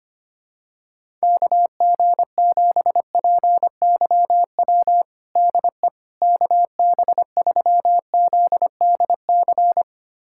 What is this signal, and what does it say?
Morse code, heard as a keyed tone, about 25 wpm, 700 Hz. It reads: KG7PYW DE KB3ZDC
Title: USA Callsign DE Callsign - 6 Characters - 25wpm